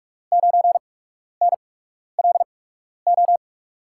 Morse code, heard as a keyed tone, about 42 wpm, spaced as for 12 wpm, 700 Hz. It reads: 9NLO
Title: USA Call Signs - 6 Characters - 8x Character Spacing - 45wpm